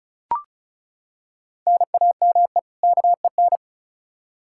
Morse code, heard as a keyed tone, about 36 wpm, 700 Hz. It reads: NAME KEN
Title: K1USN SST QSO Practice 35wpm